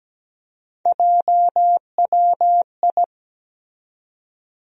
Morse code, heard as a keyed tone, about 17 wpm, 700 Hz. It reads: JWI
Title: USA Call Sign Postfixes - 3 Letters 17wpm